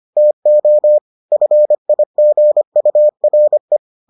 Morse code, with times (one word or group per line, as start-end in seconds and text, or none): 0.2-1.0 TO
1.3-3.8 FIGURE